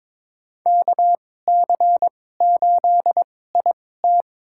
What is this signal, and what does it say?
Morse code, heard as a keyed tone, about 22 wpm, spaced as for 16 wpm, 700 Hz. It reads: KC8IT
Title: USA Call Signs - 2x Character Spacing - 22wpm